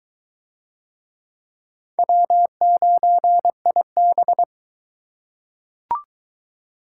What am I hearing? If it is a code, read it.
Morse code, heard as a keyed tone, about 22 wpm, 700 Hz. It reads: W9IB